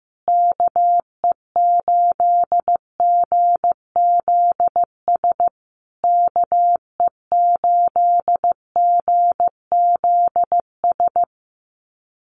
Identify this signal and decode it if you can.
Morse code, heard as a keyed tone, about 15 wpm, 700 Hz. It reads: KE8GZS KE8GZS
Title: USA Call Signs Repeated 3x 15wpm